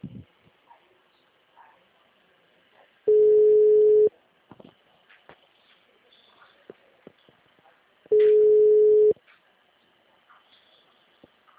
Ringback tone (Europe, 425 Hz).